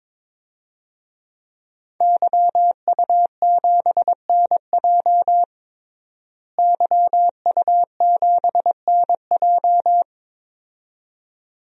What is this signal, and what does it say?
Morse code, heard as a keyed tone, about 22 wpm, 700 Hz. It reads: YU7NJ YU7NJ
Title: DXCC-100 Call Signs - 5 Characters - Repeated 2x - 22wpm